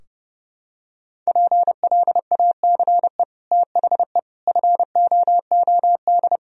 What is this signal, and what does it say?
Morse code, heard as a keyed tone, about 30 wpm, 700 Hz. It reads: PLACE THE FOOD
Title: Sentences from Top 1000 Words - Review Part 4 - 30wpm